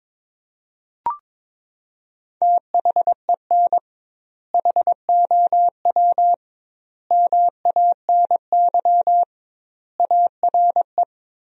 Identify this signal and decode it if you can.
Morse code, heard as a keyed tone, about 22 wpm, 700 Hz. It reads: THEN HOW MANY ARE
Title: 2x Word-Spacing - Sentences from Top 100 Words - 22wpm